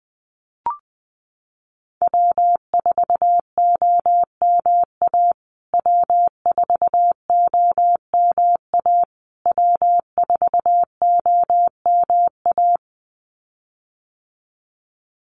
Morse code, heard as a keyed tone, about 19 wpm, 700 Hz. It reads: W4OMA W4OMA W4OMA
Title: USA Call Signs Repeated 3x 20wpm